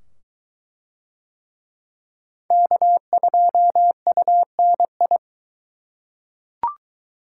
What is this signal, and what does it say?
Morse code, heard as a keyed tone, about 22 wpm, 700 Hz. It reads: K2UNI